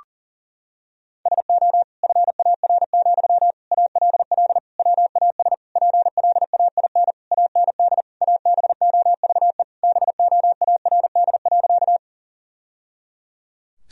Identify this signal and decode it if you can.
Morse code, heard as a keyed tone, about 40 wpm, 700 Hz. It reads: SO FAR, ALL WAS PLAIN AND ABOVE BOARD.
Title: Sentences from Top 1000 Words - Review Part 4 - 40wpm